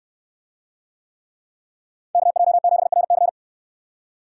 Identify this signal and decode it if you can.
Morse code, heard as a keyed tone, about 62 wpm, 700 Hz. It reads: KJ7RC